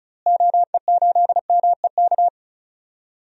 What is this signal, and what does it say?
Morse code, heard as a keyed tone, about 36 wpm, 700 Hz. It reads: OE8MEK
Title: DXCC-100 Call Signs - 6 Characters - Repeated 2x - 35wpm